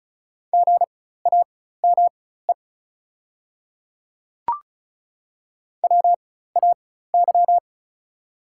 Morse code, heard as a keyed tone, about 34 wpm, spaced as for 16 wpm, 700 Hz. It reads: GAME WAY
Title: 4x Character Spacing - Top 500 Words - 35wpm